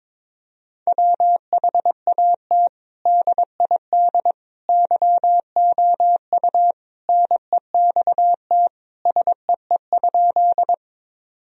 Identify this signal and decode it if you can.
Morse code, heard as a keyed tone, about 22 wpm, 700 Hz. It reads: WHAT DID YOU NEXT SEE?
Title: Sentences from Top 1000 Words - Review Part 1 - 22wpm